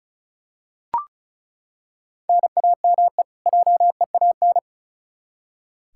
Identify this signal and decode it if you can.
Morse code, heard as a keyed tone, about 34 wpm, 700 Hz. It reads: NAME JEAN
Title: Practice QSO - Exchange 1 - Beginning - 35wpm